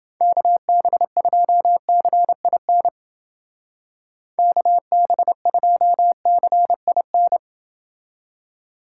Morse code, heard as a keyed tone, about 29 wpm, 700 Hz. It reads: KB2CIN KB2CIN